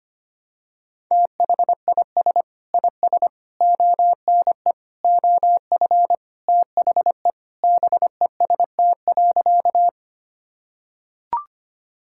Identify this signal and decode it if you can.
Morse code, heard as a keyed tone, about 25 wpm, 700 Hz. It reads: THIS IS ONE OF THE BEST.
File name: practice / 25wpm / Sentences with 4-Word Phrases - 25wpm